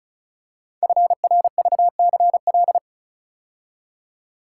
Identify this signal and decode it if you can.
Morse code, heard as a keyed tone, about 36 wpm, 700 Hz. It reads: FRVCL
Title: Code Groups - 5 Letters - 35wpm